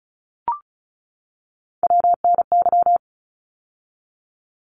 Morse code, heard as a keyed tone, about 36 wpm, 700 Hz. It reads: WNY
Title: ARRL Sections 35wpm